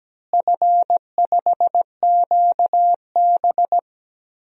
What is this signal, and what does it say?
Morse code, heard as a keyed tone, about 17 wpm, 700 Hz. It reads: F5QB
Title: DXCC-100 Call Signs - 4 Characters - Repeated 3x - 17wpm